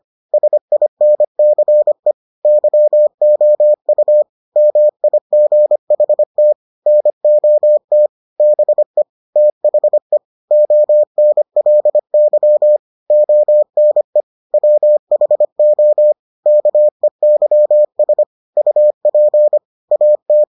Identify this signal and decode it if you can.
Morse code, heard as a keyed tone, about 25 wpm, 600 Hz. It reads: SINCE YOU MIGHT NOT BE THE ONLY ONE WHO KEYS UP AT